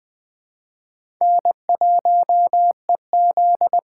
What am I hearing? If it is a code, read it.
Morse code, heard as a keyed tone, about 20 wpm, 700 Hz. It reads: N1EZ
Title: USA Call Signs - 4x Character Spacing - 20wpm